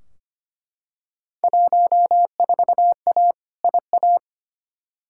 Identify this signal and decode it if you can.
Morse code, heard as a keyed tone, about 25 wpm, 700 Hz. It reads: 14A IA
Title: Field Day Exchanges Encore - 25wpm